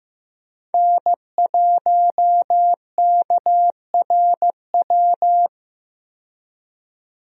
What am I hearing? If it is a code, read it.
Morse code, heard as a keyed tone, about 15 wpm, 700 Hz. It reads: N1KRW
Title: USA Call Signs - 5 Characters - 15wpm